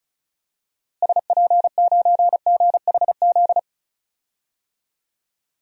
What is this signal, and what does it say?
Morse code, heard as a keyed tone, about 36 wpm, 700 Hz. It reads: SP9GHZ
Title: DXCC-100 Call Signs - 6 Characters - Repeated 2x - 35wpm